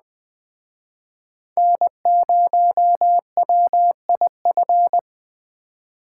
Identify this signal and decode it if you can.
Morse code, heard as a keyed tone, about 20 wpm, 700 Hz. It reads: N0WIF